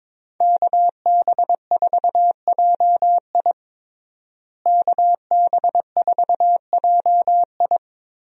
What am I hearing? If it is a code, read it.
Morse code, heard as a keyed tone, about 22 wpm, 700 Hz. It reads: KB4JI KB4JI
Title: USA Call Signs - 5 Characters - Repeated 2x - 22wpm